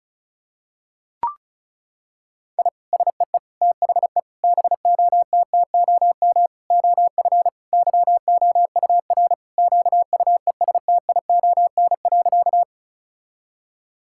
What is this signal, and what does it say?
Morse code, heard as a keyed tone, about 34 wpm, 700 Hz. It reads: I SEE THE BOTTOM OF YOUR QUESTION.
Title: Sentences from Top 1000 Words - Review Part 6 - 35wpm